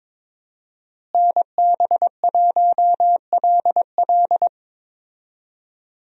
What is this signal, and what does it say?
Morse code, heard as a keyed tone, about 22 wpm, 700 Hz. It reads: NB1LL
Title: USA Call Signs - 5 Characters - Repeated 3x - 22wpm